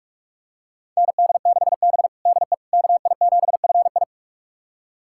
Morse code, heard as a keyed tone, about 46 wpm, 700 Hz. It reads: ND6B DE KI7FI
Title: USA Callsign DE Callsign 45wpm